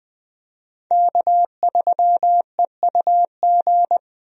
Morse code, heard as a keyed tone, about 21 wpm, 700 Hz. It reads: K3EUG
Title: USA Call Signs - 5 Characters - 20wpm